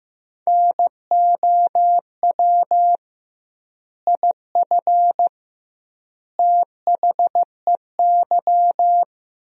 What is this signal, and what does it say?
Morse code, heard as a keyed tone, about 15 wpm, 700 Hz. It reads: NOW IF THEY
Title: 2x Word-Spacing - Sentences from Top 500 Words - 15wpm